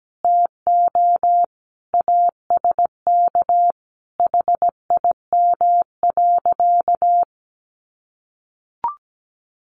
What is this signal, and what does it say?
Morse code, heard as a keyed tone, about 17 wpm, 700 Hz. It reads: TO ASK HIM.
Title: Sentences from Top 1000 Words 17wpm